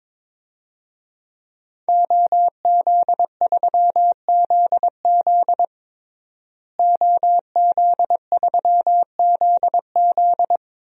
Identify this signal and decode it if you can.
Morse code, heard as a keyed tone, about 22 wpm, 700 Hz. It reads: OZ3ZZ OZ3ZZ
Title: DXCC-100 Call Signs - 5 Characters - Repeated 3x - 22wpm